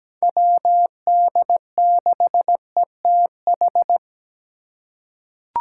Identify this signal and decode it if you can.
Morse code, heard as a keyed tone, about 17 wpm, 700 Hz. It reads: WD6ETH
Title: USA Call Signs - 6 Characters - 2x Character Spacing - 17wpm